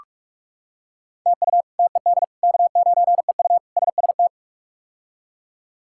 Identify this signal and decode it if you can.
Morse code, heard as a keyed tone, about 46 wpm, 700 Hz. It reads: TU TED K9EU SST
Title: K1USN SST QSO Practice - 45wpm